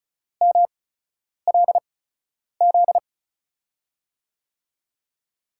Morse code, heard as a keyed tone, about 36 wpm, spaced as for 10 wpm, 700 Hz. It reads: MLZ
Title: USA Call Signs - 8x Character Spacing - 35wpm